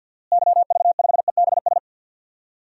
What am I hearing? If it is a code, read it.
Morse code, heard as a keyed tone, about 53 wpm, 700 Hz. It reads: YV5EBS
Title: DXCC-100 Call Signs - 6 Characters - Repeated 2x - 50wpm